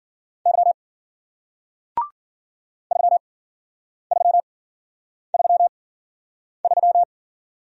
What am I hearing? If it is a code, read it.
Morse code, heard as a keyed tone, about 54 wpm, spaced as for 9 wpm, 700 Hz. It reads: Y 3333